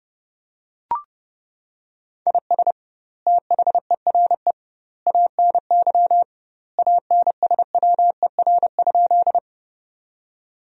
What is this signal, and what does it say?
Morse code, heard as a keyed tone, about 30 wpm, 700 Hz. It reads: IS THERE ANY ANSWER?
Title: 2x Word-Spacing - Sentences from Top 300 Words - 30wpm